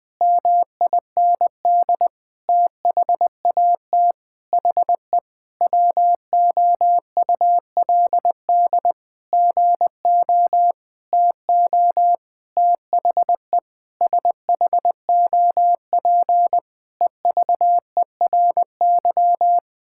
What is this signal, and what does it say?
Morse code, heard as a keyed tone, about 20 wpm, 700 Hz. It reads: MIND THAT HE WOULD GO TO THE SHOP EVERY
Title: Sentences from Top 1600 Words 20wpm